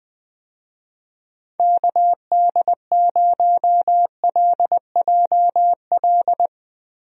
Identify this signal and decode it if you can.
Morse code, heard as a keyed tone, about 20 wpm, 700 Hz. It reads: KD0LJL
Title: USA Call Signs - 6 Characters - 20wpm